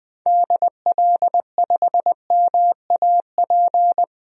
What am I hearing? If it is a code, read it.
Morse code, heard as a keyed tone, about 20 wpm, 700 Hz. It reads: DL5MAP
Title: DXCC-100 Call Signs - 6 Characters - 20wpm